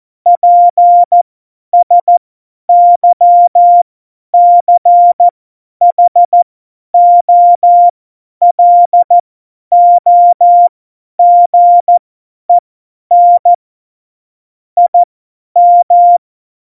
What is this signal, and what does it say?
Morse code, heard as a keyed tone, about 14 wpm, spaced as for 10 wpm, 700 Hz. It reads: PSYCHOLOGEN IM
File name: Morsekurs / Witze / Witz 27. Februar 2018, 50 F70